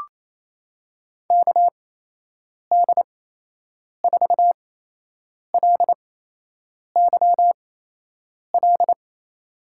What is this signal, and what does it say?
Morse code, heard as a keyed tone, about 28 wpm, spaced as for 8 wpm, 700 Hz. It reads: KD4LYL
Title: USA Call Signs - 6 Characters - 8x Character Spacing - 28wpm